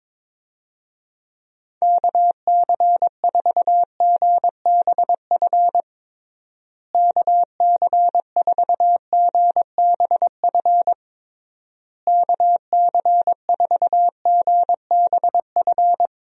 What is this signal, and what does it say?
Morse code, heard as a keyed tone, about 22 wpm, 700 Hz. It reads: KC4GBF KC4GBF KC4GBF
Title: USA Call Signs - 6 Characters - Repeated 3x - 22wpm